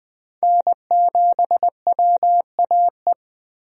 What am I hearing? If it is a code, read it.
Morse code, heard as a keyed tone, about 21 wpm, 700 Hz. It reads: N7WAE
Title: USA Call Signs - 5 Characters - Repeated 2x - 20wpm